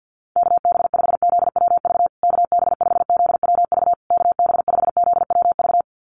Morse code, heard as a keyed tone, about 50 wpm, 700 Hz. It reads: KB5ZWV KB5ZWV KB5ZWV
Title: USA Call Signs Repeated 3x 50wpm